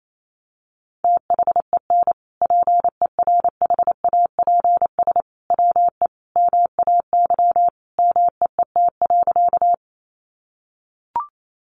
Morse code, heard as a keyed tone, about 28 wpm, 700 Hz. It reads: THEN PERHAPS WE MAY MEET.
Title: Sentences from Top 1100 Words 28wpm